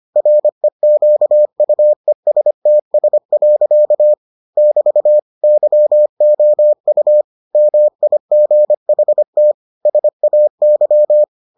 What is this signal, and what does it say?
Morse code, heard as a keyed tone, about 25 wpm, 600 Hz. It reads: REQUESTS. <BT> YOU MIGHT SAY